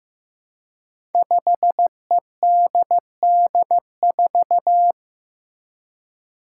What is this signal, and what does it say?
Morse code, heard as a keyed tone, about 15 wpm, 700 Hz. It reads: 5EDD4
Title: Five Challenging Letter-Number Combinations 15wpm